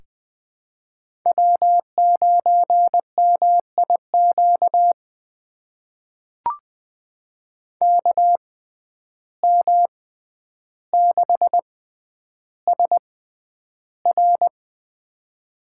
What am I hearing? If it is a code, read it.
Morse code, heard as a keyed tone, about 20 wpm, spaced as for 10 wpm, 700 Hz. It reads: W9MIQ KM6SR
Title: USA Call Signs - 5 Characters - 6x Character Spacing - 20wpm